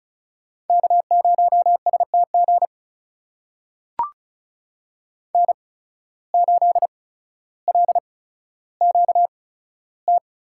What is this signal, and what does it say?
Morse code, heard as a keyed tone, about 34 wpm, spaced as for 15 wpm, 700 Hz. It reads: K0STG N8LQT
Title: USA Call Signs - 8x Character Spacing - 35wpm